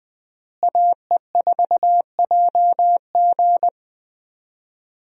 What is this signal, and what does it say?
Morse code, heard as a keyed tone, about 21 wpm, 700 Hz. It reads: AE4JG
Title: USA Call Signs - 5 Characters - Repeated 3x - 20wpm